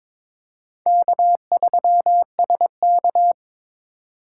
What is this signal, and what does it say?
Morse code, heard as a keyed tone, about 22 wpm, 700 Hz. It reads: K3SK
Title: USA Call Signs - 4 Characters - 8x Character Spacing - 22wpm